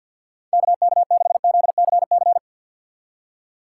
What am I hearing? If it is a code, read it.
Morse code, heard as a keyed tone, about 63 wpm, 700 Hz. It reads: KK6ZCC